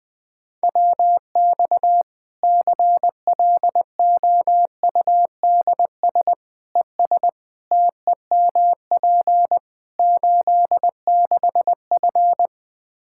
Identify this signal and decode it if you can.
Morse code, heard as a keyed tone, about 21 wpm, 700 Hz. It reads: WX CLOUDS ES TEMP 86F
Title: Practice QSO - Exchange 2 - Info - 20wpm